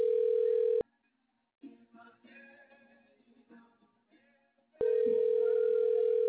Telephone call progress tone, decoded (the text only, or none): ringback tone